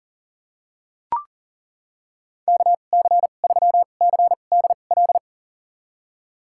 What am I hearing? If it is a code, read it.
Morse code, heard as a keyed tone, about 40 wpm, spaced as for 29 wpm, 700 Hz. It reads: KC3CDL
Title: USA Call Signs - 6 Characters - 2x Character Spacing - 40wpm